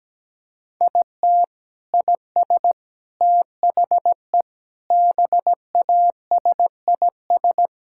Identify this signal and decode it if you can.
Morse code, heard as a keyed tone, about 17 wpm, 700 Hz. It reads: IT IS THE BASIS
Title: Sentences from Top 1000 Words - Review Part 5 - 17wpm